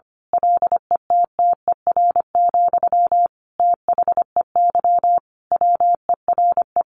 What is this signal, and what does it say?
Morse code, heard as a keyed tone, about 25 wpm, 700 Hz. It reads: LETTER, THEY WERE
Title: Sentences from Top 1500 Words 25wpm